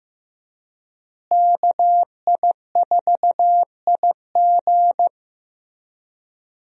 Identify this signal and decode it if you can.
Morse code, heard as a keyed tone, about 15 wpm, 700 Hz. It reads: KI4IG
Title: USA Call Signs - 5 Characters - Repeated 3x - 15wpm